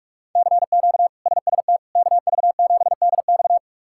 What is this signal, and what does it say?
Morse code, heard as a keyed tone, about 47 wpm, 700 Hz. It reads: CQ SST KV7DX
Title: K1USN SST QSO Practice - 45wpm